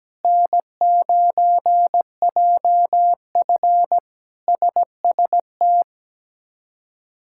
Morse code, heard as a keyed tone, about 17 wpm, 700 Hz. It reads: N9JF SST
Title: K1USN SST QSO Practice - 17wpm